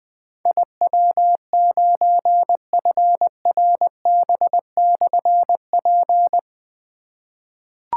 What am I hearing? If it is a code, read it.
Morse code, heard as a keyed tone, about 21 wpm, 700 Hz. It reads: IW9FRB/P
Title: Difficult Call Signs 20wpm